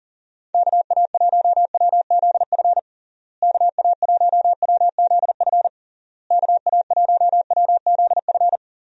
Morse code, heard as a keyed tone, about 37 wpm, 700 Hz. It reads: KA1WZF KA1WZF KA1WZF
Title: USA Call Signs - 6 Characters - Repeated 3x - 40wpm